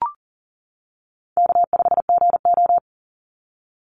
Morse code, heard as a keyed tone, about 38 wpm, 700 Hz. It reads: K5GO
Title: Easy Call Signs - 40wpm